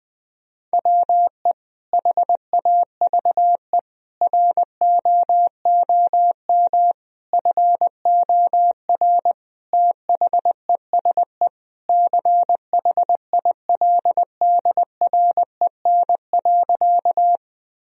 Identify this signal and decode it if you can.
Morse code, heard as a keyed tone, about 21 wpm, 700 Hz. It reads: WE HAVE ROOM FOR THESE CHILDREN.
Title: Sentences from Top 1000 Words - Review Part 1 - 20wpm